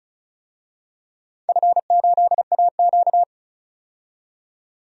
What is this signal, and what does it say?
Morse code, heard as a keyed tone, about 34 wpm, 700 Hz. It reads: F8AQ
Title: DXCC-100 Call Signs - 4 Characters - 35wpm